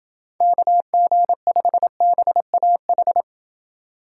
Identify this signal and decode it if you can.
Morse code, heard as a keyed tone, about 27 wpm, 700 Hz. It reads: KG5BAH